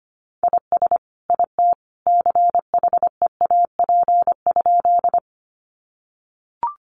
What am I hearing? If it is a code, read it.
Morse code, heard as a keyed tone, about 25 wpm, 700 Hz. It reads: IS IT CHEAP?
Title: Sentences from Top 1200 Words 25wpm